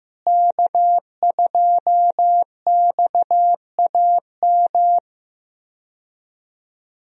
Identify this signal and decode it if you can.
Morse code, heard as a keyed tone, about 15 wpm, 700 Hz. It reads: K2XAM
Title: USA Call Signs - 5 Characters - Repeated 3x - 15wpm